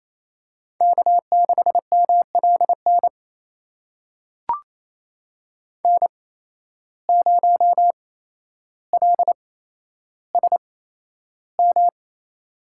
Morse code, heard as a keyed tone, about 28 wpm, spaced as for 12 wpm, 700 Hz. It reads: K6MLN N0LSM
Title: USA Call Signs - 5 Characters - 8x Character Spacing - 28wpm